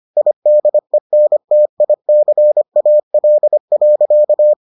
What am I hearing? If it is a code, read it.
Morse code, heard as a keyed tone, about 25 wpm, 600 Hz. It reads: IDENTICAL.